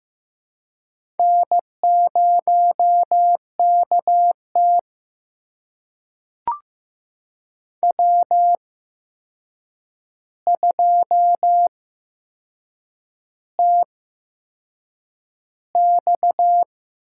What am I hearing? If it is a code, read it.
Morse code, heard as a keyed tone, about 15 wpm, spaced as for 6 wpm, 700 Hz. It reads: N0KT W2TX
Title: USA Call Signs - 4 Characters - 8x Character Spacing - 15wpm